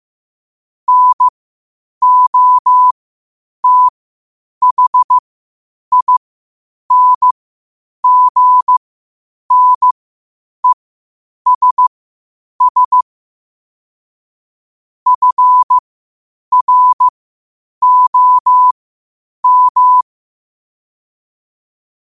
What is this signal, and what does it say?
Morse code, heard as a keyed tone, about 15 wpm, spaced as for 8 wpm, 1000 Hz. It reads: NOTHINGNESS FROM
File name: qotdfiles / 8wpm